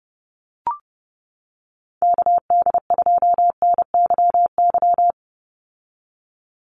Morse code, heard as a keyed tone, about 30 wpm, 700 Hz. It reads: KD2NYY
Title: USA Call Signs 30wpm